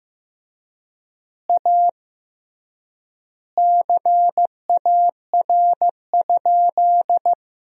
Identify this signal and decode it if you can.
Morse code, heard as a keyed tone, about 15 wpm, 700 Hz. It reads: A CAR?
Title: 3x Word-Spacing - Sentences from Top 300 Words - 15wpm